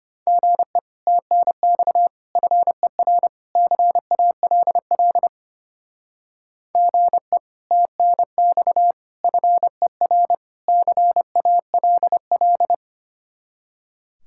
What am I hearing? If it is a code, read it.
Morse code, heard as a keyed tone, about 32 wpm, 700 Hz. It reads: GE TNX FER CALL GE TNX FER CALL